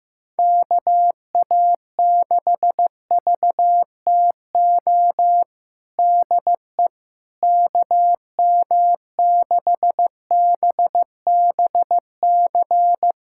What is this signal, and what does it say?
Morse code, heard as a keyed tone, about 15 wpm, 700 Hz. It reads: KA6VTO DE KM6BBC
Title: USA Callsign DE Callsign 15wpm